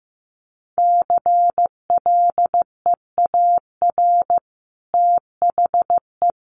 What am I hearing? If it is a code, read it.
Morse code, heard as a keyed tone, about 15 wpm, 700 Hz. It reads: CLEAR THE
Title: Sentences from Top 1000 Words 15wpm